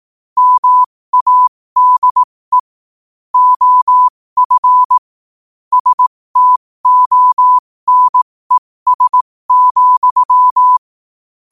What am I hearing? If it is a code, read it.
Morse code, heard as a keyed tone, about 18 wpm, spaced as for 15 wpm, 1000 Hz. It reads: MADE OF STONES,
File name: qotdfiles / 15wpm